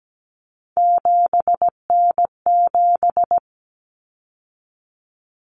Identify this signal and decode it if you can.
Morse code, heard as a keyed tone, about 17 wpm, 700 Hz. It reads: 7N7
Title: Common Cut Numbers - 3-Digit 17wpm